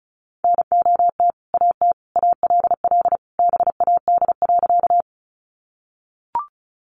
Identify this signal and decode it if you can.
Morse code, heard as a keyed tone, about 34 wpm, 700 Hz. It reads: NOT AT ALL BAD.
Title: Sentences from Top 700 Words 35wpm